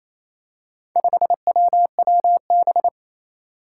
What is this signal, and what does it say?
Morse code, heard as a keyed tone, about 28 wpm, 700 Hz. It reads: 5WWB
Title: Four Challenging Letter-Number Combinations 28wpm